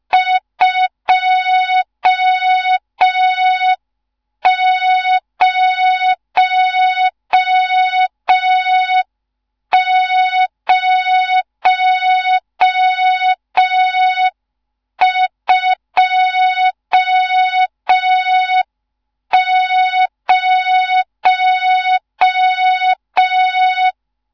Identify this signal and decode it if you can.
Morse code, heard as a keyed tone, about 5 wpm, 3700 Hz. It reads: 20020
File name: Morze-HA2MN / hang / 14 280 HA2MN